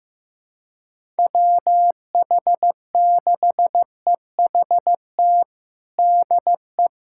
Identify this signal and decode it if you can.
Morse code, heard as a keyed tone, about 15 wpm, 700 Hz. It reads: WH6EHT DE
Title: USA Callsign DE Callsign 15wpm